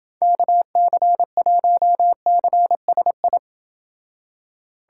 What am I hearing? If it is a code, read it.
Morse code, heard as a keyed tone, about 29 wpm, 700 Hz. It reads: KC1CSI